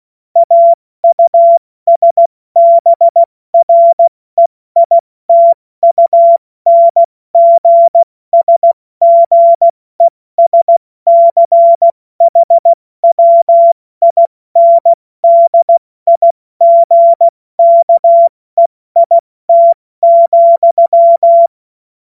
Morse code, heard as a keyed tone, about 16 wpm, 680 Hz. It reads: AUSBREITUNGSGESCHWINDIGKEIT,
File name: Morsekurs / Witze / Witz 14. Juni 2022 70 F80